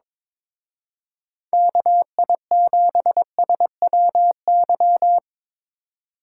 Morse code, heard as a keyed tone, about 26 wpm, 700 Hz. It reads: KI7SWY
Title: Speed Racing - USA 2x3 Call Signs 22wpm